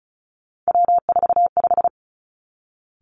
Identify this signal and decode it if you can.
Morse code, heard as a keyed tone, about 36 wpm, 700 Hz. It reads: W45
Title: Three Challenging Letter-Number Combinations 35wpm